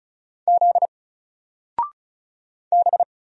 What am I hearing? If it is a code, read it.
Morse code, heard as a keyed tone, about 34 wpm, 700 Hz. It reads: Z B
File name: Single-Letter-Number-35wpm